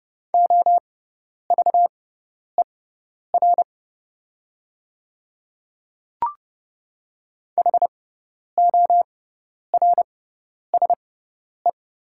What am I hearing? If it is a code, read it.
Morse code, heard as a keyed tone, about 30 wpm, spaced as for 10 wpm, 700 Hz. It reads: OVER HORSE
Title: 6x Character Spacing - Top 300 Words - 30wpm